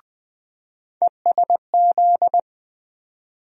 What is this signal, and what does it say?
Morse code, heard as a keyed tone, about 21 wpm, 700 Hz. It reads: ESZ
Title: USA Call Sign Postfixes - 3 Letters 20wpm